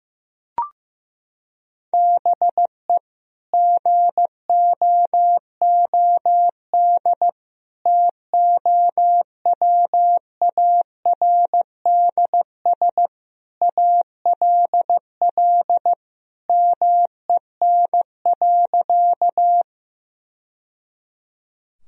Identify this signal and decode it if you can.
Morse code, heard as a keyed tone, about 15 wpm, 700 Hz. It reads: BE GOOD TOWARDS ALL MEN.
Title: Sentences from Top 1000 Words - Review Part 2 - 15wpm